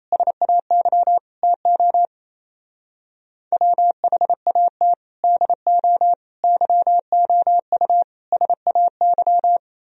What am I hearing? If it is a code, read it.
Morse code, heard as a keyed tone, about 35 wpm, 700 Hz. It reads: SAY TO WHAT DO YOU SAY